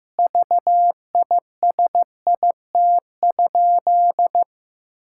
Morse code, heard as a keyed tone, about 15 wpm, 700 Hz. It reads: VISIT?
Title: Sentences from Top 1600 Words 15wpm